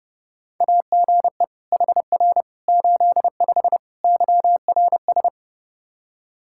Morse code, heard as a keyed tone, about 30 wpm, 700 Hz. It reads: AGE HR 85 YRS
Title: Practice QSO - Exchange 3 - More Info - 30wpm